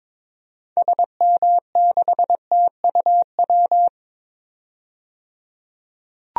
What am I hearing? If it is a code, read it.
Morse code, heard as a keyed tone, about 22 wpm, 700 Hz. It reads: SM6TUW
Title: DXCC-100 Call Signs - 6 Characters - 22wpm